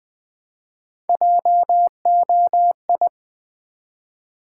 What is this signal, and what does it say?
Morse code, heard as a keyed tone, about 20 wpm, 700 Hz. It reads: JOI
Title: Build A Word - 1 Letter At A Time - 5-Letter Words - 20wpm